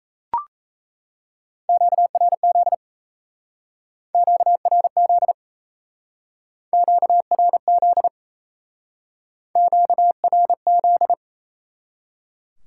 Morse code, heard as a keyed tone, about 37 wpm, 700 Hz. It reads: QRZ QRZ QRZ QRZ